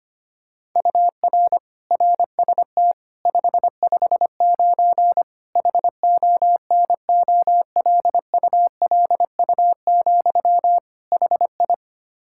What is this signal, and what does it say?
Morse code, heard as a keyed tone, about 25 wpm, 700 Hz. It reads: UR RST 559 HONOLULU, HI
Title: Practice QSO - Exchange 1 - Beginning - 25wpm